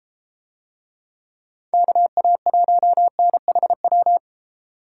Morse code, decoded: KA1NHW

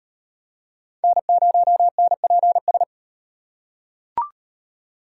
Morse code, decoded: N0NPS

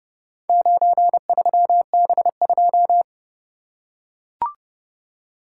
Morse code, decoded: 93B2

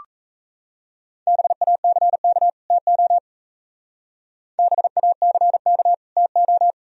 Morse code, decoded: BACK TO BACK TO

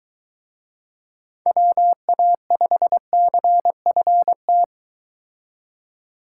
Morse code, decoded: WA5CFT

